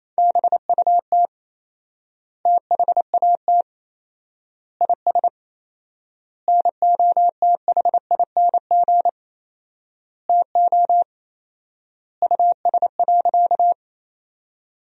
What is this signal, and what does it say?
Morse code, heard as a keyed tone, about 28 wpm, 700 Hz. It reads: BUT THAT IS NOTHING TO US.
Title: 4x Word-Spacing - Sentences from Top 500 Words - 28wpm